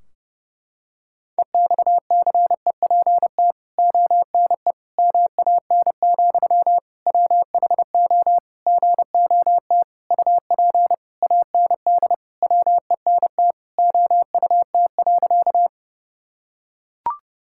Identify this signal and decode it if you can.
Morse code, heard as a keyed tone, about 30 wpm, 700 Hz. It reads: EXCEPT ONE MAN, WHO GOT UP AND WENT OUT.
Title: Sentences from Top 1600 Words 30wpm